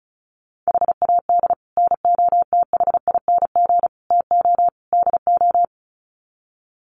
Morse code, heard as a keyed tone, about 34 wpm, 700 Hz. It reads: HAD NOTHING TO DO
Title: Sets of 4 Words 35wpm